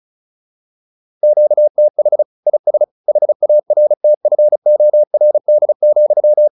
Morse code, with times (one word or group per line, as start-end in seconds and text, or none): 1.2-2.2 QTH
2.5-2.8 IS
3.1-6.5 HARTFORD,